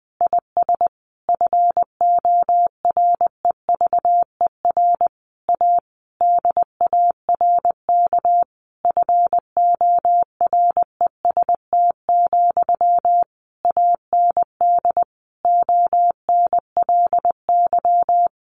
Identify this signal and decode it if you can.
Morse code, heard as a keyed tone, about 19 wpm, 700 Hz. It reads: IS FOREVER A DARK FOREST, AND ONLY
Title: A Sound Of Thunder 20wpm